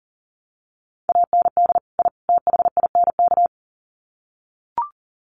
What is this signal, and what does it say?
Morse code, heard as a keyed tone, about 40 wpm, 700 Hz. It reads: AND I THINK
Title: Sets of 3 Words 40wpm